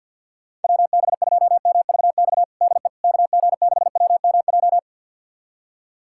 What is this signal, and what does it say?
Morse code, heard as a keyed tone, about 52 wpm, 700 Hz. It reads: WB2MVX DE KG6WMJ
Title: USA Callsign DE Callsign - 6 Characters - 50wpm